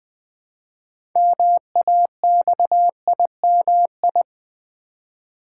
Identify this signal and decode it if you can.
Morse code, heard as a keyed tone, about 20 wpm, 700 Hz. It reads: MAXIMI
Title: Build A Word - 2 Letters At A Time - 8-Letter Words - 20wpm